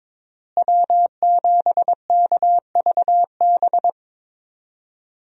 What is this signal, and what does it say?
Morse code, heard as a keyed tone, about 22 wpm, 700 Hz. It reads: W7KVB
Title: USA Call Signs - 5 Characters - Repeated 3x - 22wpm